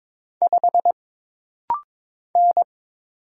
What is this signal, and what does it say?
Morse code, heard as a keyed tone, about 22 wpm, 700 Hz. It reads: 5 N
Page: This practice set is composed of Letters, Numbers, common punctuations, and common prosigns.